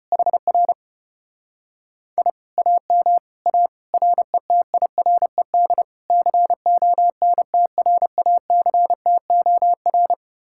Morse code, heard as a keyed tone, about 33 wpm, 700 Hz. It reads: HR I AM A RETIRED CONTRACTOR